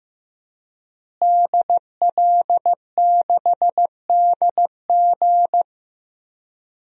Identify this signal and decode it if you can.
Morse code, heard as a keyed tone, about 15 wpm, 700 Hz. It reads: DL6DG
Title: DXCC-100 Call Signs - 5 Characters - Repeated 3x - 15wpm